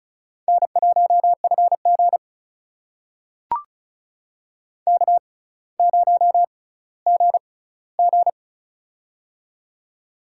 Morse code, heard as a keyed tone, about 34 wpm, spaced as for 18 wpm, 700 Hz. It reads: N1FG K0GG
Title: USA Call Signs - 4 Characters - 6x Character Spacing - 35wpm